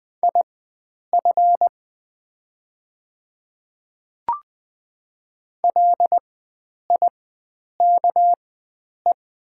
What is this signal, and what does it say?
Morse code, heard as a keyed tone, about 20 wpm, spaced as for 9 wpm, 700 Hz. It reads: IF LIKE
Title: 4x Character Spacing - Top 100 Words - 20wpm